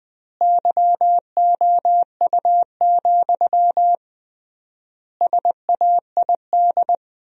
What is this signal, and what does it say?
Morse code, heard as a keyed tone, about 21 wpm, 700 Hz. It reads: YOU, SAID
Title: 3x Word-Spacing - Sentences from Top 400 Words - 20wpm